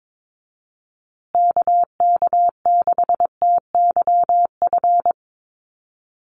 Morse code, heard as a keyed tone, about 22 wpm, 700 Hz. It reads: KK6TYF
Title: USA Call Signs Repeated 2x 22wpm